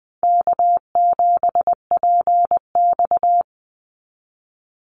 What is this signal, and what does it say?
Morse code, heard as a keyed tone, about 20 wpm, 700 Hz. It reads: K7PX
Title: NA SOTA Activator Callsigns 20wpm